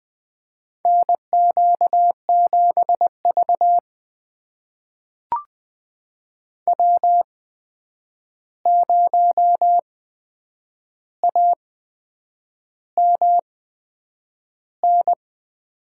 Morse code, heard as a keyed tone, about 20 wpm, spaced as for 8 wpm, 700 Hz. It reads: NQ7V W0AMN
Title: USA Call Signs - 8x Character Spacing - 20wpm